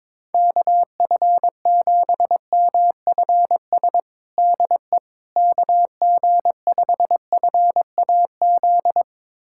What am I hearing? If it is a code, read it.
Morse code, heard as a keyed tone, about 22 wpm, 700 Hz. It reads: KF7MFS DE KG5FAZ
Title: USA Callsign DE Callsign - 6 Characters - 22wpm